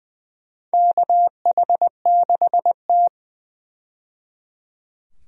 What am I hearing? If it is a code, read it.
Morse code, heard as a keyed tone, about 20 wpm, 700 Hz. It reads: KH6T
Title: USA Call Signs - 4 Characters - 20wpm